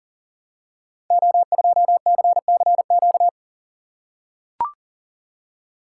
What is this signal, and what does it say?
Morse code, heard as a keyed tone, about 38 wpm, 700 Hz. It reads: O2CCQ
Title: Code Groups - 5 Letter:Numbers - 40wpm